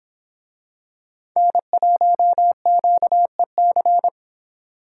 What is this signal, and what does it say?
Morse code, heard as a keyed tone, about 24 wpm, 700 Hz. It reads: N1QEC